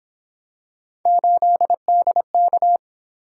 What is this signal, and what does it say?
Morse code, heard as a keyed tone, about 28 wpm, 700 Hz. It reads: 8DK